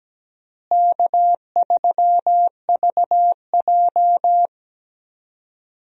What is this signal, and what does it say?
Morse code, heard as a keyed tone, about 17 wpm, 700 Hz. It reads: K3VJ
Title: USA Call Signs - 4 Characters - Repeated 3x - 17wpm